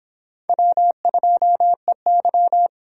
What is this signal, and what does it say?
Morse code, heard as a keyed tone, about 23 wpm, 700 Hz. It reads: W2EY